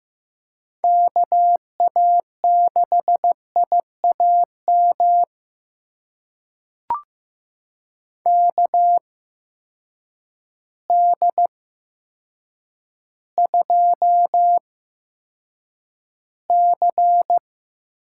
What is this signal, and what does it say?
Morse code, heard as a keyed tone, about 15 wpm, spaced as for 8 wpm, 700 Hz. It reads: KA6IAM KD2C
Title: USA Call Signs - 6 Characters - 8x Character Spacing - 15wpm